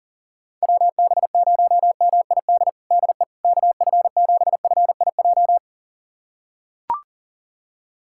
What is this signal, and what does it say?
Morse code, heard as a keyed tone, about 40 wpm, 700 Hz. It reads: WB0MID DE KF7FIJ
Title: USA Callsign DE Callsign - 6 Characters - 40wpm